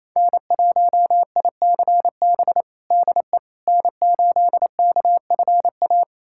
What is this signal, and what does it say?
Morse code, heard as a keyed tone, about 28 wpm, 700 Hz. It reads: N1ICB DE N8KFA
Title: USA Callsign DE Callsign - 5 Characters - 28wpm